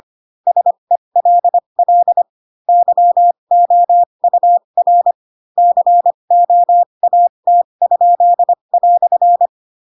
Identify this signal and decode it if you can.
Morse code, heard as a keyed tone, about 25 wpm, spaced as for 22 wpm, 700 Hz. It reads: SELL YOUR COAT?"